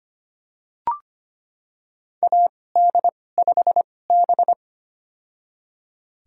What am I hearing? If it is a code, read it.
Morse code, heard as a keyed tone, about 25 wpm, spaced as for 18 wpm, 700 Hz. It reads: AD5B
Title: USA Call Signs - 4 Characters - 2x Character Spacing - 25wpm